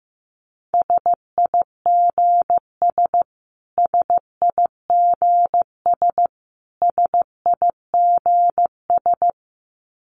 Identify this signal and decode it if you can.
Morse code, heard as a keyed tone, about 15 wpm, 700 Hz. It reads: SIGS SIGS SIGS
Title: Abbreviations Repeated 3x - 15wpm